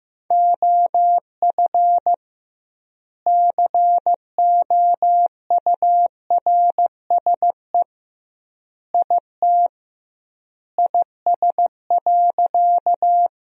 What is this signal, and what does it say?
Morse code, heard as a keyed tone, about 16 wpm, 700 Hz. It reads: OF COURSE IT IS.
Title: 2x Word-Spacing - Sentences from Top 500 Words - 15wpm